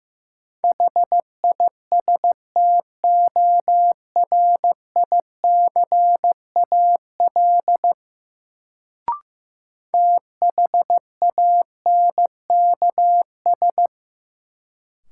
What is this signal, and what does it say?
Morse code, heard as a keyed tone, about 15 wpm, 700 Hz. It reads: HISTORICAL THANKS
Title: Sentences from Top 600 Words - New Word Drill - 15wpm